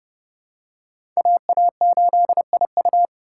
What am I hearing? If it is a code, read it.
Morse code, heard as a keyed tone, about 30 wpm, 700 Hz. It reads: AA8IU
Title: USA Call Signs - 5 Characters - 2x Character Spacing - 30wpm